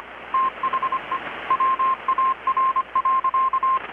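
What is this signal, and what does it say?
Morse code, heard as a keyed tone, about 25 wpm, 1060 Hz, with radio noise. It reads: THE WAR.
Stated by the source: Morse from KPH's Night of Nights XX maritime radiotelegraphy event, received in Chiapas, Mexico about 3,800 km away on a Kenwood R-600 with a grounded 10 m long-wire antenna.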